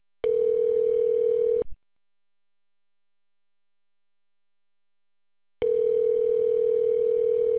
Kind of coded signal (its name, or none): telephone call progress tone